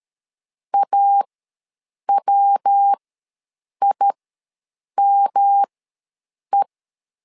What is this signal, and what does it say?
Morse code, heard as a keyed tone, about 13 wpm, spaced as for 7 wpm, 800 Hz. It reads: AWIME
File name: LEZIONI / 3 MWRO / 3 MWRO ws 12wpm 1914